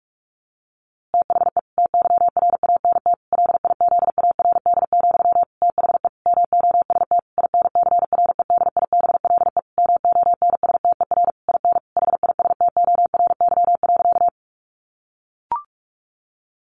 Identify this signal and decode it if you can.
Morse code, heard as a keyed tone, about 44 wpm, 700 Hz. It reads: THE TYRANT LIZARD, THE MOST INCREDIBLE MONSTER IN HISTORY.
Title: A Sound Of Thunder 45wpm